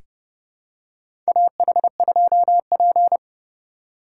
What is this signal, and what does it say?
Morse code, heard as a keyed tone, about 30 wpm, 700 Hz. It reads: AH2P
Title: USA Call Signs - 4 Characters - 4x Character Spacing - 30wpm